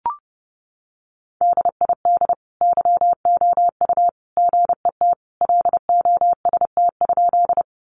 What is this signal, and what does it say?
Morse code, heard as a keyed tone, about 30 wpm, 700 Hz. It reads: DID YOU GET LOST?
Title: Sentences from Top 1400 Words 30wpm